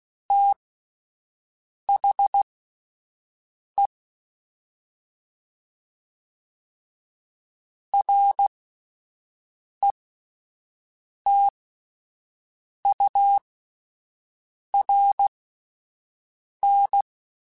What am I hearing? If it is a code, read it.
Morse code, heard as a keyed tone, about 16 wpm, spaced as for 5 wpm, 800 Hz. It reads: THE RETURN
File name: (spaced) / QSOs Spaced-mp3s / kc7haa @10 sp